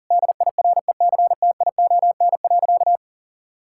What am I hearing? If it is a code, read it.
Morse code, heard as a keyed tone, about 37 wpm, 700 Hz. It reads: DIRECTION.
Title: Sentences from Top 1000 Words - Review Part 5 - 40wpm